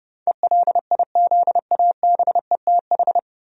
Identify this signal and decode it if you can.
Morse code, heard as a keyed tone, about 30 wpm, 700 Hz. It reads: ELIZABETH
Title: USA Common Names - 30wpm